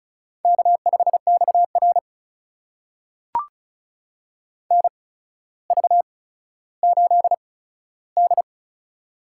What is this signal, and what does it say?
Morse code, heard as a keyed tone, about 36 wpm, spaced as for 15 wpm, 700 Hz. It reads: K5XR NV8D
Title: USA Call Signs - 4 Characters - 8x Character Spacing - 35wpm